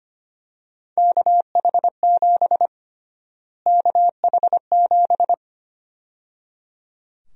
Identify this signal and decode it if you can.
Morse code, heard as a keyed tone, about 25 wpm, 700 Hz. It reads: KH7 KH7
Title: USA Call Sign Prefixes - Repeated 2x - 25wpm